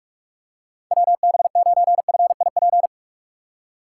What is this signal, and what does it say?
Morse code, heard as a keyed tone, about 46 wpm, 700 Hz. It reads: WB9FIP
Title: K1USN SST QSO Practice - 45wpm